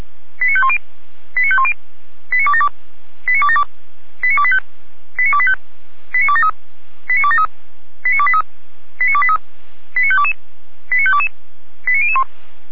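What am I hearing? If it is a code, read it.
Five-tone selective call (ZVEI), heard as quick runs of five tones, tones of about 70 ms. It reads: 89510 89510 89272 89272 89285 89285 89273 89273 89283 89283 89522 89522 89001
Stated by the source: ZVEI five-tone call 89220 received by SDR, from the audio server of Feuerwehr Orlamünde.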